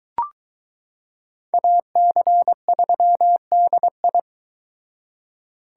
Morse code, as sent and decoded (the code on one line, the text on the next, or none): .- -.-. ...-- -.. ..
AC3DI